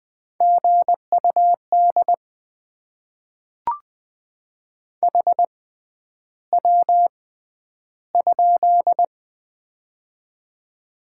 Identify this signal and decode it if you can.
Morse code, heard as a keyed tone, about 20 wpm, spaced as for 10 wpm, 700 Hz. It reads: GUD HW?
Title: Top 20 QSO Elements - 6x Character Spacing - 20wpm